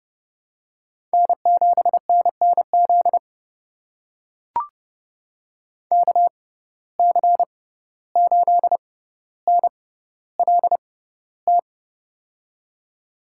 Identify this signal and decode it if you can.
Morse code, heard as a keyed tone, about 30 wpm, spaced as for 15 wpm, 700 Hz. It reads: N7NNZ KC8NLT
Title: USA Call Signs - 6x Character Spacing - 30wpm